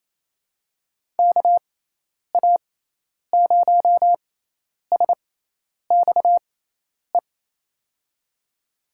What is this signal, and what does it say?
Morse code, heard as a keyed tone, about 28 wpm, spaced as for 10 wpm, 700 Hz. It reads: KA0SXE
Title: USA Call Signs - 6x Character Spacing - 28wpm